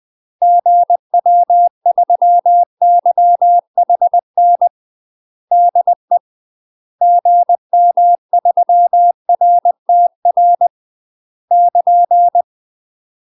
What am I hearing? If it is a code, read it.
Morse code, heard as a keyed tone, about 20 wpm, 700 Hz. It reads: GW3YHN DE GM3RTR <KN>
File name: UK MASTER QSOs TRAINING / 20WPM / UK MASTER QSOs 0051